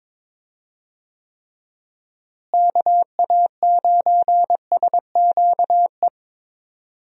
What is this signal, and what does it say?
Morse code, heard as a keyed tone, about 22 wpm, 700 Hz. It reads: KA9SQE
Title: USA Call Signs - 6 Characters - Repeated 2x - 22wpm